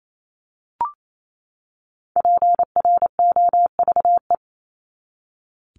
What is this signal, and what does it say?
Morse code, heard as a keyed tone, about 28 wpm, 700 Hz. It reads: PROVE
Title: ED Suffix - Top 100 Words with Base Words 28wpm